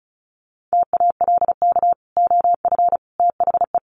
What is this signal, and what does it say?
Morse code, heard as a keyed tone, about 34 wpm, 700 Hz. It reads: TALK OF THE
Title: Sets of 4 Words 35wpm